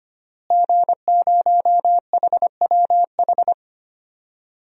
Morse code, heard as a keyed tone, about 25 wpm, 700 Hz. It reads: G0HWH
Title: DXCC-100 Call Signs - 5 Characters - Repeated 3x - 25wpm